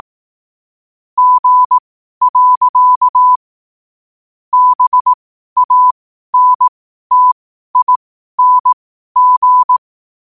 Morse code, heard as a keyed tone, about 18 wpm, spaced as for 12 wpm, 1000 Hz. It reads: G. BANTING
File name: qotdfiles / 12wpm